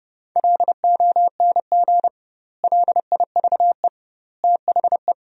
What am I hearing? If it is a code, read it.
Morse code, heard as a keyed tone, about 30 wpm, 700 Hz. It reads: LONG LIVE THE
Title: 2x Word-Spacing - Sentences from Top 400 Words - 30wpm